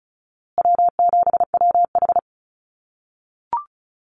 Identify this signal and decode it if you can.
Morse code, heard as a keyed tone, about 34 wpm, 700 Hz. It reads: W7WH
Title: USA Call Signs Repeated 2x 35wpm